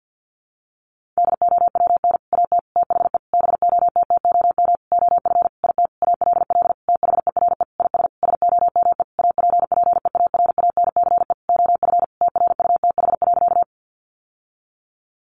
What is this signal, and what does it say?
Morse code, heard as a keyed tone, about 50 wpm, 700 Hz. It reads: DOWN AT THE BOTTOM OF IT ALL THERE IS SOME APPEARANCE OF TRUTH.
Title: Sentences from Top 1100 Words 50wpm